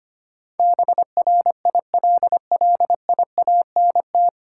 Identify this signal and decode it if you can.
Morse code, heard as a keyed tone, about 25 wpm, 700 Hz. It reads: BRILLIANT
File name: practice / 25wpm / 9 Letter Words - First 3 Letters Given - 25wpm